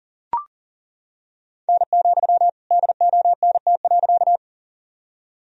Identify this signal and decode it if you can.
Morse code, heard as a keyed tone, about 38 wpm, 700 Hz. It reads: N, DONT.
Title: Sentences from Top 300 Words and Abbreviations 40wpm